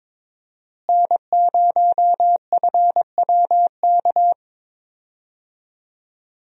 Morse code, heard as a keyed tone, about 22 wpm, 700 Hz. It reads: N0FWK
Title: USA Call Signs - 5 Characters - Repeated 2x - 22wpm